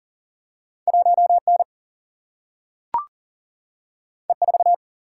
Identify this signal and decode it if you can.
Morse code, heard as a keyed tone, about 39 wpm, 700 Hz. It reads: 1 N E 4
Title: Single-Letter Single-Number 40wpm